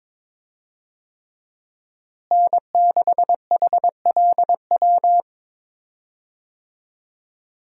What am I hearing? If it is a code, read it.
Morse code, heard as a keyed tone, about 22 wpm, 700 Hz. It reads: N6HLW
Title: USA Call Signs - 5 Characters - Repeated 3x - 22wpm